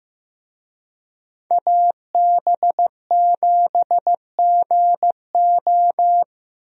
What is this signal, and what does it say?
Morse code, heard as a keyed tone, about 15 wpm, 700 Hz. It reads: AB7GO
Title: USA Call Signs - 5 Characters - Repeated 2x - 15wpm